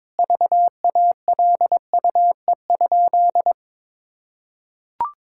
Morse code, heard as a keyed tone, about 22 wpm, 700 Hz. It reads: VALUE?
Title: Sentences from Top 1000 Words - Review Part 6 - 22wpm